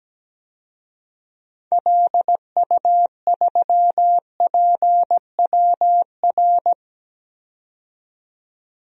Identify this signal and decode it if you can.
Morse code, heard as a keyed tone, about 17 wpm, 700 Hz. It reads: LU3PWR
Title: DXCC-100 Call Signs - 6 Characters - 17wpm